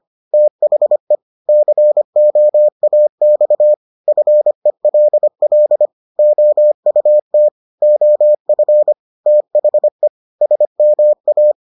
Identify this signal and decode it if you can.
Morse code, heard as a keyed tone, about 25 wpm, 600 Hz. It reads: THE COAX FELL OUT OF THE SMA